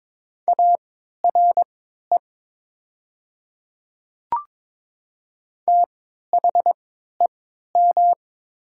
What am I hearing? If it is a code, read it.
Morse code, heard as a keyed tone, about 22 wpm, spaced as for 12 wpm, 700 Hz. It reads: ARE THEM
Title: 3x Character Spacing - Top 100 Words - 22wpm